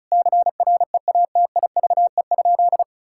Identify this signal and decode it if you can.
Morse code, heard as a keyed tone, about 36 wpm, 700 Hz. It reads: CREATIVE?
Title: Sentences from Top 1000 Words - Review Part 5 - 35wpm